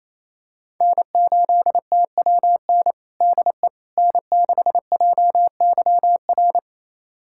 Morse code, heard as a keyed tone, about 28 wpm, 700 Hz. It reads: N8TWN DE N6JYR
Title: USA Callsign DE Callsign 28wpm